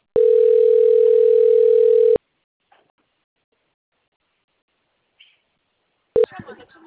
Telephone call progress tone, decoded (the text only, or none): ringback tone